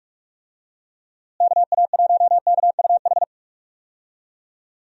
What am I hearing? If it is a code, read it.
Morse code, heard as a keyed tone, about 41 wpm, 700 Hz. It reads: KA1KUH
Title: USA Call Signs - 6 Characters - Repeated 3x - 45wpm